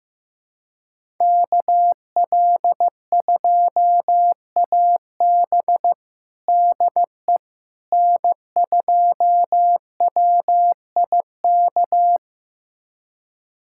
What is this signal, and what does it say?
Morse code, heard as a keyed tone, about 15 wpm, 700 Hz. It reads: KL2AB DE N2WIK
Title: USA Callsign DE Callsign - 5 Characters - 15wpm